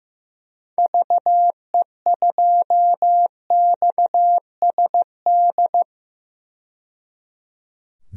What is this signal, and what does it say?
Morse code, heard as a keyed tone, about 15 wpm, 700 Hz. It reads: VE2XSD
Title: DXCC-100 Call Signs - 6 Characters - 15wpm